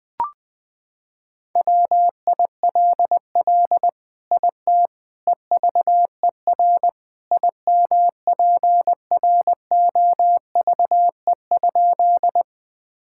Morse code, heard as a keyed tone, about 21 wpm, 700 Hz. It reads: WILL IT EVER IMPROVE?
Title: Sentences from Top 1000 Words - Review Part 5 - 20wpm